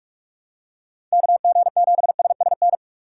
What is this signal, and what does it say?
Morse code, heard as a keyed tone, about 46 wpm, 700 Hz. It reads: KG7SSN
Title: USA Call Signs - 6 Characters - 4x Character Spacing - 45wpm